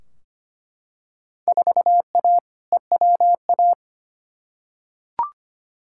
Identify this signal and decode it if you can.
Morse code, heard as a keyed tone, about 25 wpm, 700 Hz. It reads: 4A EWA
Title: Field Day Exchanges Encore - 25wpm